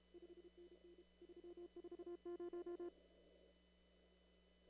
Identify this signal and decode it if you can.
Morse code, heard as a keyed tone, about 32 wpm, 350 Hz. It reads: 5NN 340
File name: CW / kt5j wpx cw-op-k5pi